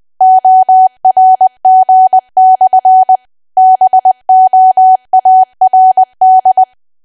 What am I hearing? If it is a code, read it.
Morse code, heard as a keyed tone, about 20 wpm, 750 Hz. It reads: ORG/BOARD
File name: cwa / W1AW-Level3 / 20WPM